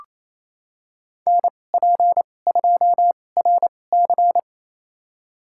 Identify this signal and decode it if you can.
Morse code, heard as a keyed tone, about 28 wpm, spaced as for 20 wpm, 700 Hz. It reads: NP2RC
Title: USA Call Signs - 5 Characters - 2x Character Spacing - 28wpm